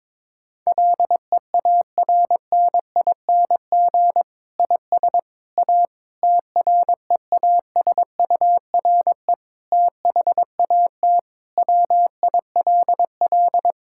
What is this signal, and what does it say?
Morse code, heard as a keyed tone, about 22 wpm, 700 Hz. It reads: LEARNING IS A TREASURE THAT WILL
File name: Difficult Sentences with Hints - 22wpm